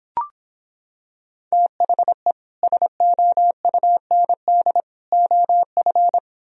Morse code, heard as a keyed tone, about 26 wpm, 700 Hz. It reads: THE SOUND OF